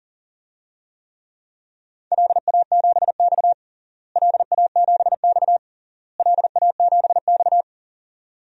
Morse code, heard as a keyed tone, about 40 wpm, 700 Hz. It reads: LA7X LA7X LA7X
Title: DXCC-100 Call Signs - 4 Characters - Repeated 3x - 40wpm